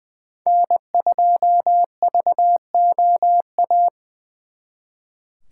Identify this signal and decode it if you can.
Morse code, heard as a keyed tone, about 25 wpm, 700 Hz. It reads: N2VOA
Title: Speed Racing - USA 1x3 Call Signs 20wpm